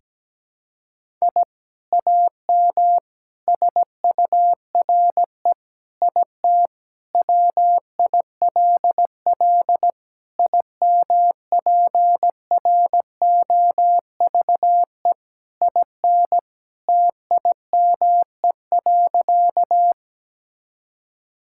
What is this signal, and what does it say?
Morse code, heard as a keyed tone, about 17 wpm, 700 Hz. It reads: I AM SURE IT WILL IMPROVE IN TIME.
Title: Sentences from Top 1000 Words - Review Part 5 - 17wpm